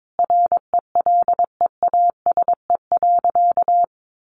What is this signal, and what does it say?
Morse code, heard as a keyed tone, about 22 wpm, 700 Hz. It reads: RELEASE.
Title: Sentences from Top 1100 Words 22wpm